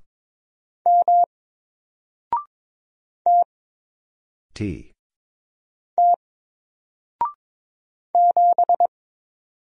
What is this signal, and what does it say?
Morse code, heard as a keyed tone, about 21 wpm, 700 Hz. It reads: M T T 7
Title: Single Letter-Number 22wpm